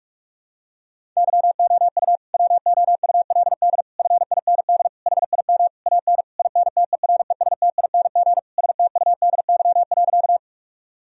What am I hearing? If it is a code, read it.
Morse code, heard as a keyed tone, about 46 wpm, 700 Hz. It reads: YOU WOULD FIND HIM AN INTERESTING STUDY.
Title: Sentences from Top 1000 Words - Review Part 1 - 45wpm